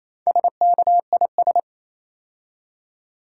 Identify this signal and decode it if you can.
Morse code, heard as a keyed tone, about 27 wpm, 700 Hz. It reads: SKIS
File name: Easy Words - Speed-Racing with Hints - 22wpm